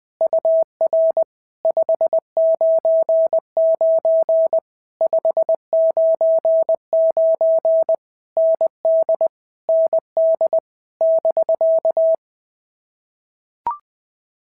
Morse code, heard as a keyed tone, about 20 wpm, 650 Hz. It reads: UR 599 599 ND ND <BK>
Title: POTA QSO Practice - 20wpm